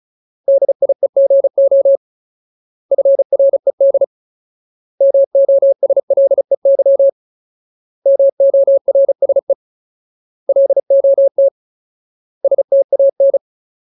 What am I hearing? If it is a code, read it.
Morse code, heard as a keyed tone, about 34 wpm, 550 Hz. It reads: DIEGO FRED MOSLEY MORSE LOT STAN
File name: Ultimate Word List / 35WPM Ultimate